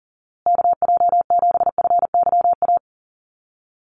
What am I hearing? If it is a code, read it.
Morse code, heard as a keyed tone, about 40 wpm, 700 Hz. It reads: KJ7FYA
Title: USA Call Signs Repeated 2x 40wpm